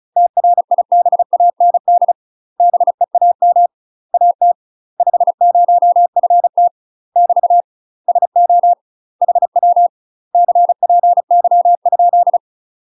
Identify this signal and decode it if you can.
Morse code, heard as a keyed tone, about 34 wpm, 700 Hz. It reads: TRIBAND BEAM AT 50FT <BT> SO HW CPY ?
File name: UK MASTER QSOs TRAINING / 35WPM / UK MASTER QSOs 0000